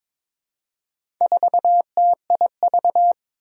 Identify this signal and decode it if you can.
Morse code, heard as a keyed tone, about 22 wpm, 700 Hz. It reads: 4TIV
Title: Four Challenging Letter-Number Combinations 22wpm